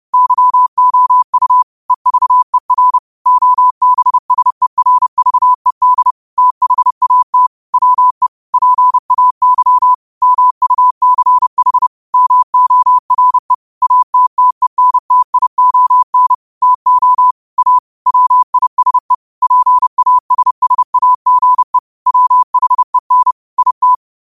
Morse code, heard as a keyed tone, about 30 wpm, 1000 Hz. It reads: YOU EVER OBSERVED THAT WE PAY MUCH MORE ATTENTION TO A WISE PASSAGE WHEN IT
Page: Quotes for Wed, 13 Aug 2025 in Morse Code at 30 words per minute.